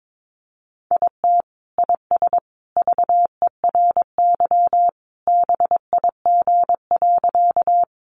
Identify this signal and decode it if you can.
Morse code, heard as a keyed tone, about 22 wpm, 700 Hz. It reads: IT IS VERY BIG.
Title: Easy Phrases 22wpm